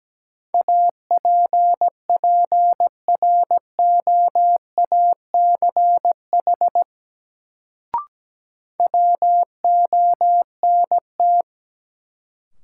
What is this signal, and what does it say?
Morse code, heard as a keyed tone, about 17 wpm, 700 Hz. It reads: APPROACH WONT
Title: Sentences from Top 1000 Words - Review Part 6 - New Word Drill - 17wpm